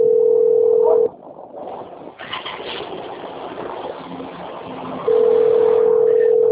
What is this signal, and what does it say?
Ringback tone (North America), 440 + 480 Hz.
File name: out- - - - - .22056